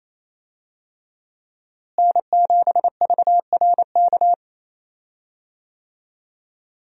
Morse code, heard as a keyed tone, about 28 wpm, 700 Hz. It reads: N7VRK
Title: USA Call Signs - 5 Characters - Repeated 2x - 28wpm